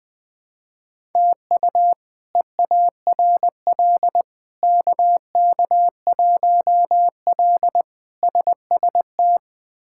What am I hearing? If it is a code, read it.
Morse code, heard as a keyed tone, about 20 wpm, 700 Hz. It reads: TU EARL KK1L SST
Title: K1USN SST QSO Practice - 20wpm